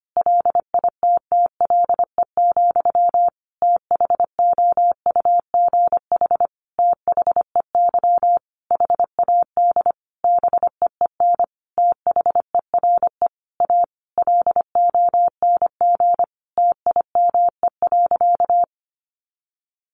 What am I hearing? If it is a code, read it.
Morse code, heard as a keyed tone, about 25 wpm, 700 Hz. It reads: LITTLE, THOUGH THEY HAD BEEN THERE A LONG TIME.
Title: Sentences from Top 400 Words 25wpm